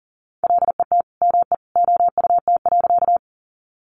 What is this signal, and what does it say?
Morse code, heard as a keyed tone, about 40 wpm, 700 Hz. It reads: LET ME OUT.
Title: Sentences from Top 300 Words 40wpm